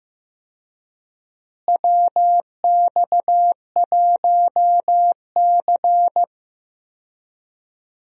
Morse code, heard as a keyed tone, about 15 wpm, 700 Hz. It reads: WX1C
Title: USA Call Signs - 4 Characters - Repeated 2x - 15wpm